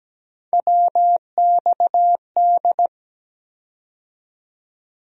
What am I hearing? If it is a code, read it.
Morse code, heard as a keyed tone, about 17 wpm, 700 Hz. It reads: WXD
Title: USA Call Sign Postfixes - 3 Letters 17wpm